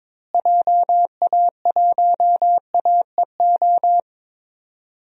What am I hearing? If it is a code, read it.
Morse code, heard as a keyed tone, about 22 wpm, 700 Hz. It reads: JA1AEO
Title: DXCC-100 Call Signs - 6 Characters - Repeated 2x - 22wpm